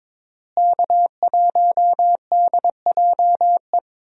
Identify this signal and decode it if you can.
Morse code, heard as a keyed tone, about 22 wpm, 700 Hz. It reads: K1DJE
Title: USA Call Signs - 5 Characters - Repeated 3x - 22wpm